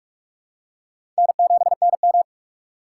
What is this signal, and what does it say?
Morse code, heard as a keyed tone, about 44 wpm, 700 Hz. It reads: N7NM